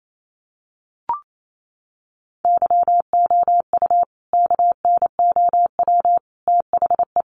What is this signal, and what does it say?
Morse code, heard as a keyed tone, about 28 wpm, 700 Hz. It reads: YOU KNOW THE
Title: Sentences from Top 1000 Words 28wpm